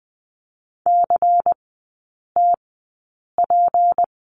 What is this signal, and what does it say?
Morse code, heard as a keyed tone, about 20 wpm, spaced as for 8 wpm, 700 Hz. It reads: CTP
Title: 152 Lesson 25e - Letter p - Three Characters 20wpm